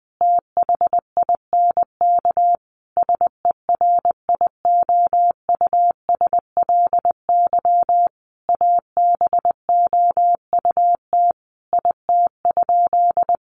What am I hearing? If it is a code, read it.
Morse code, heard as a keyed tone, about 20 wpm, 700 Hz. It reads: THINK SERIOUSLY ABOUT IT?
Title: Sentences from Top 1200 Words 20wpm